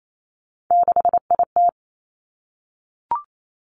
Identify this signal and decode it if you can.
Morse code, heard as a keyed tone, about 28 wpm, 700 Hz. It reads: 6IT
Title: Three Challenging Letter-Number Combinations 28wpm